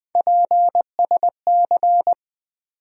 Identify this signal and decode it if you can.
Morse code, heard as a keyed tone, about 20 wpm, 700 Hz. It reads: PSC
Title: USA Call Sign Postfixes - 3 Letters 20wpm